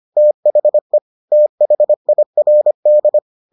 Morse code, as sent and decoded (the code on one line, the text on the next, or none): - .... / - .... .. .-. -..
THE THIRD